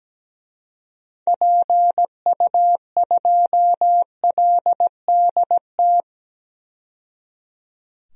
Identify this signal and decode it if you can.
Morse code, heard as a keyed tone, about 17 wpm, 700 Hz. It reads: PU2LDT
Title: DXCC-100 Call Signs - 6 Characters - 17wpm